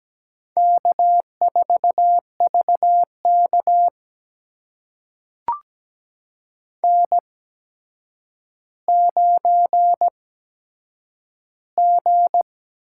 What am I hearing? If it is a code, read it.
Morse code, heard as a keyed tone, about 17 wpm, spaced as for 8 wpm, 700 Hz. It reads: K4VK N9G
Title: USA Call Signs - 4 Characters - 8x Character Spacing - 17wpm